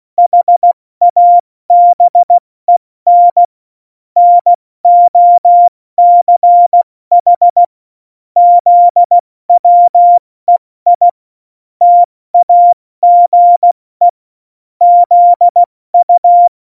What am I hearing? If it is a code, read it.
Morse code, heard as a keyed tone, about 16 wpm, 700 Hz. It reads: HABEN NOCH ZWEI TAGE ZU
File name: Morsekurs / Witze / Witz 22. September 2020, 70 F80